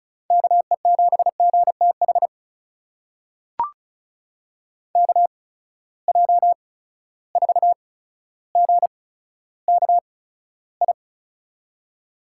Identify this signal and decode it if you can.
Morse code, heard as a keyed tone, about 34 wpm, spaced as for 15 wpm, 700 Hz. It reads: KE7GTH KJ4GKI
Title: USA Call Signs - 8x Character Spacing - 35wpm